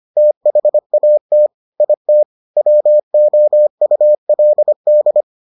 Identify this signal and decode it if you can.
Morse code, heard as a keyed tone, about 25 wpm, 600 Hz. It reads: THAT IT WOULD